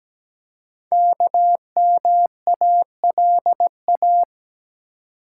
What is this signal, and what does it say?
Morse code, heard as a keyed tone, about 17 wpm, 700 Hz. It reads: KMALA
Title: Code Groups - 5 Letters - 17wpm